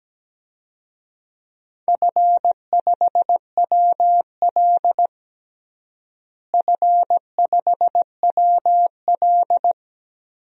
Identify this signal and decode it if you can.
Morse code, heard as a keyed tone, about 17 wpm, 700 Hz. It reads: F5WL F5WL
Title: DXCC-100 Call Signs - 4 Characters - Repeated 2x - 17wpm